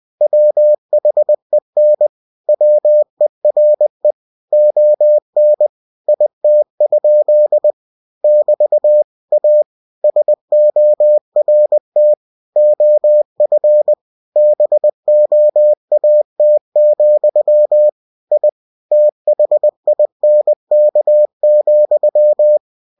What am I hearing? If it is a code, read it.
Morse code, heard as a keyed tone, about 20 wpm, 600 Hz. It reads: WHEN WERE ON IT? <BT> A SORT OF BOAT, I THINK,